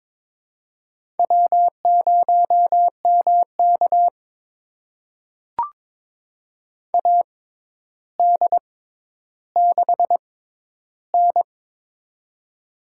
Morse code, heard as a keyed tone, about 21 wpm, spaced as for 11 wpm, 700 Hz. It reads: W0MK AD6N
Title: USA Call Signs - 4 Characters - 6x Character Spacing - 22wpm